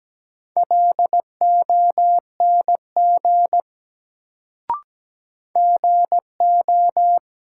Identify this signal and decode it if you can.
Morse code, heard as a keyed tone, about 17 wpm, 700 Hz. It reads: LONG GO
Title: Sets of 2 Words - New Word Drill - 17wpm